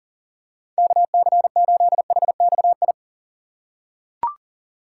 Morse code, heard as a keyed tone, about 40 wpm, 700 Hz. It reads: KC8HXI
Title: USA Call Signs - 6 Characters - 4x Character Spacing - 40wpm